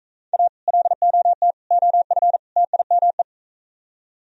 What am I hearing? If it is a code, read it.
Morse code, heard as a keyed tone, about 43 wpm, 700 Hz. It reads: A LOT OF TIME